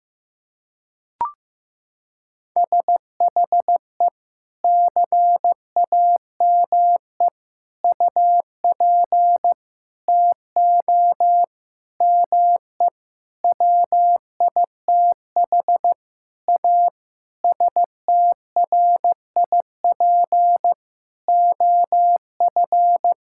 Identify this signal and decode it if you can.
Morse code, heard as a keyed tone, about 15 wpm, 700 Hz. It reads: SHE CAME UP TO ME WITH A STRIP OF
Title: Sentences from Top 1600 Words 15wpm